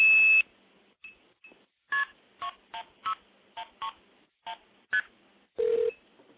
Multi-frequency trunk signalling, heard as fast pairs of tones, tones of about 90 ms. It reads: KP16131ST